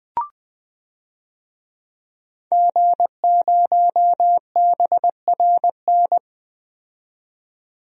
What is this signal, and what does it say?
Morse code, heard as a keyed tone, about 20 wpm, 700 Hz. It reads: G0BRN
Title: DXCC-100 Call Signs - 5 Characters - 20wpm